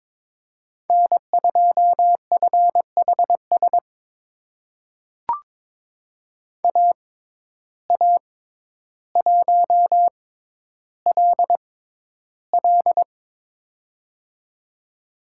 Morse code, heard as a keyed tone, about 22 wpm, spaced as for 11 wpm, 700 Hz. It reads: N2FHS AA1LL
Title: USA Call Signs - 5 Characters - 6x Character Spacing - 22wpm